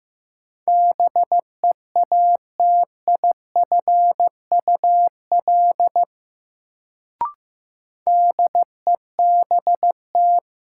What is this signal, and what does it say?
Morse code, heard as a keyed tone, about 15 wpm, 700 Hz. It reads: BEATIFUL DEBT
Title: Sentences from Top 1300 Words - New Word Drill - 15wpm